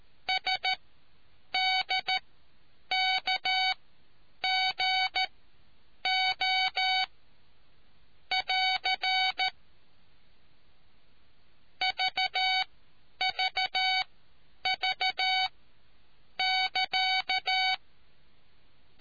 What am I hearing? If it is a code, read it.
Morse code, heard as a keyed tone, about 13 wpm, spaced as for 8 wpm, 750 Hz. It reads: SDKGO <AR> VVV<KA>